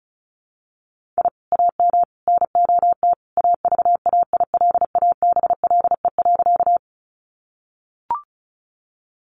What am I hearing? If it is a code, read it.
Morse code, heard as a keyed tone, about 34 wpm, 700 Hz. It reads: I AM NOT AVAILABLE.
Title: Sentences from Top 600 Words 35wpm